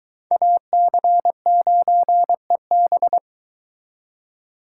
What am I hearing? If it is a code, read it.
Morse code, heard as a keyed tone, about 22 wpm, 700 Hz. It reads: AC9EB